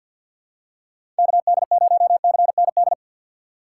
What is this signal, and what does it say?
Morse code, heard as a keyed tone, about 53 wpm, 700 Hz. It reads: KD0CND
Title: USA Call Signs - 6 Characters - 3x Character Spacing - 50wpm